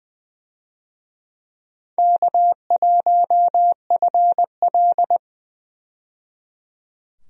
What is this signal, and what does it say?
Morse code, heard as a keyed tone, about 21 wpm, 700 Hz. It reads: K1FL
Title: USA Call Signs - 4 Characters - 20wpm